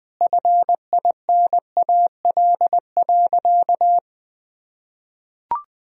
Morse code, heard as a keyed tone, about 20 wpm, 700 Hz. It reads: FINAL.
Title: Sentences from Top 1000 Words - Review Part 2 - 20wpm